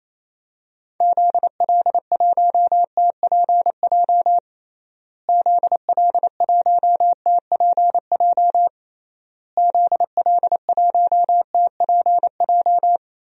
Morse code, heard as a keyed tone, about 28 wpm, 700 Hz. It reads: ZL1TPJ ZL1TPJ ZL1TPJ
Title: DXCC-100 Call Signs - 6 Characters - Repeated 3x - 28wpm